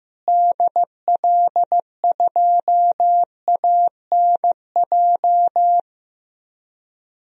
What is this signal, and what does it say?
Morse code, heard as a keyed tone, about 15 wpm, 700 Hz. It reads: DL2ANJ
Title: DXCC-100 Call Signs - 6 Characters - Repeated 3x - 15wpm